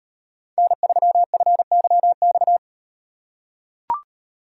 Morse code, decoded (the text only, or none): N3FYX